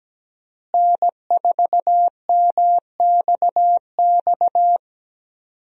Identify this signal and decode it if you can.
Morse code, heard as a keyed tone, about 17 wpm, 700 Hz. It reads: N4MXX
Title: USA Call Signs - 5 Characters - Repeated 3x - 17wpm